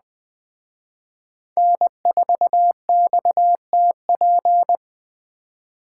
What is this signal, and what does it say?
Morse code, heard as a keyed tone, about 20 wpm, 700 Hz. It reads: N4XTP
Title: Speed Racing - USA Call Signs 17wpm